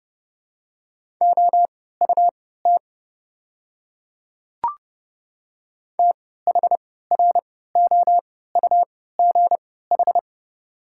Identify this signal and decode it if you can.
Morse code, heard as a keyed tone, about 32 wpm, spaced as for 17 wpm, 700 Hz. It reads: OUT THROUGH
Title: 3x Character Spacing - Top 200 Words - 30wpm